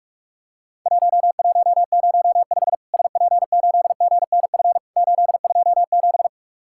Morse code, heard as a keyed tone, about 44 wpm, 700 Hz. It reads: 1105 SP8GNF 827
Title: Worked All Europe DX Contest - QTC Exchange - 45wpm